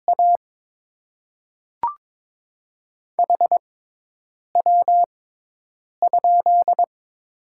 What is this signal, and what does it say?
Morse code, heard as a keyed tone, about 22 wpm, spaced as for 8 wpm, 700 Hz. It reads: A HW?
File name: Top 20 QSO Elements - 6x Character Spacing - 22wpm